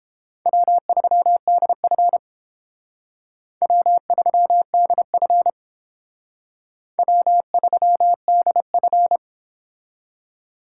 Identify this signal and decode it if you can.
Morse code, heard as a keyed tone, about 30 wpm, 700 Hz. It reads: W3DF W3DF W3DF